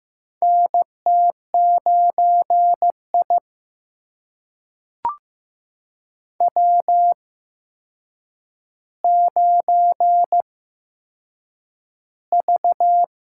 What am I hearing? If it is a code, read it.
Morse code, heard as a keyed tone, about 15 wpm, spaced as for 7 wpm, 700 Hz. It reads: NT9I W9V
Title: USA Call Signs - 4 Characters - 8x Character Spacing - 15wpm